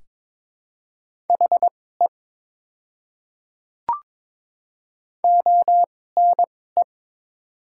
Morse code, heard as a keyed tone, about 21 wpm, spaced as for 16 wpm, 690 Hz. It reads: HE ONE
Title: 2x Character Spacing - Top 100 Words - 22wpm